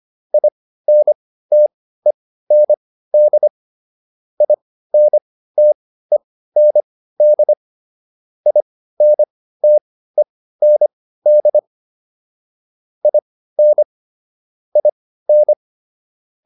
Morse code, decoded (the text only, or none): INTEND INTEND INTEND IN IN